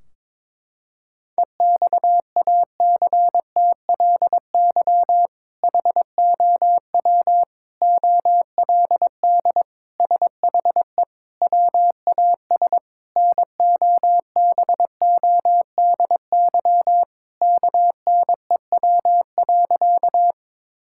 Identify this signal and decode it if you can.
Morse code, heard as a keyed tone, about 22 wpm, 700 Hz. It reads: EXACTLY HOW OLD SHE WAS NOBODY KNEW.
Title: Sentences from Top 1600 Words 22wpm